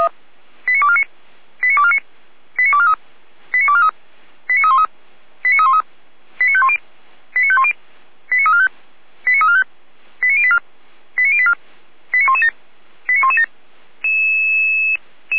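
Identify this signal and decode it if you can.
Five-tone selective call (ZVEI), heard as quick runs of five tones, tones of about 70 ms. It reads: 89249 89249 89243 89243 89313 89313 89510 89510 89345 89345 89084 89084 89117 89117
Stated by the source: ZVEI five-tone selective call received on SDR, from the feuerwehr-orlamuende.org fire brigade audio server.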